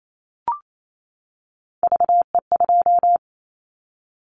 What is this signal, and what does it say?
Morse code, heard as a keyed tone, about 28 wpm, 700 Hz. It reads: VE2
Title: Callsign Prefixes for Top 100 DXCC - 28wpm